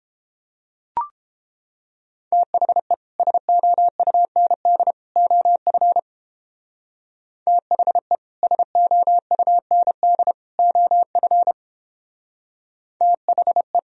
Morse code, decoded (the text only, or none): THE SOUND OF THE SOUND OF THE